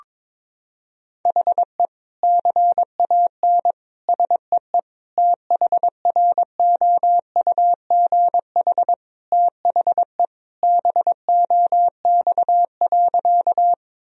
Morse code, heard as a keyed tone, about 22 wpm, 700 Hz. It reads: HE CAN SEE THROUGH THE BOX.
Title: Sentences from Top 1000 Words - Review Part 2 - 22wpm